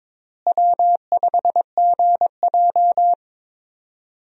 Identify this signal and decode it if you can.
Morse code, heard as a keyed tone, about 21 wpm, 700 Hz. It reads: W5GJ
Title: USA Call Signs - 4 Characters - Repeated 3x - 22wpm